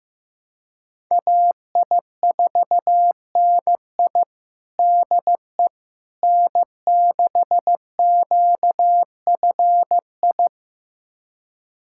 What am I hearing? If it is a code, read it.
Morse code, heard as a keyed tone, about 15 wpm, 700 Hz. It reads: AI4NI DE N6QFI
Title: USA Callsign DE Callsign - 5 Characters - 15wpm